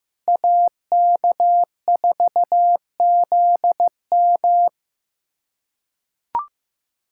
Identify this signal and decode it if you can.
Morse code, heard as a keyed tone, about 15 wpm, 700 Hz. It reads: AK4ZM
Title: USA Callsign DE Callsign - 5 Characters - 15wpm